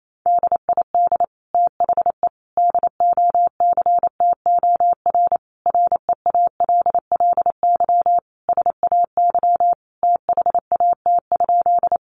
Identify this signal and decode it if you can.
Morse code, heard as a keyed tone, about 28 wpm, 700 Hz. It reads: DID THE DOCTOR REALLY SAY THAT?
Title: Sentences from Top 1500 Words 28wpm